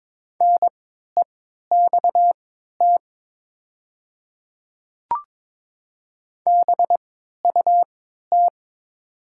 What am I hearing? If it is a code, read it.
Morse code, heard as a keyed tone, about 22 wpm, spaced as for 12 wpm, 700 Hz. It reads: NEXT BUT
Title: 3x Character Spacing - Top 300 Words - 22wpm